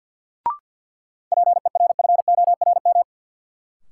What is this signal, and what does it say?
Morse code, heard as a keyed tone, about 51 wpm, 700 Hz. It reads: PERFORM
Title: Sentences from Top 1500 Words - New Word Drill Alt - 50wpm